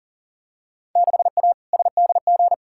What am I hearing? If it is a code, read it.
Morse code, heard as a keyed tone, about 40 wpm, 700 Hz. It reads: 6A SDG
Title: Field Day Exchanges Encore - 40wpm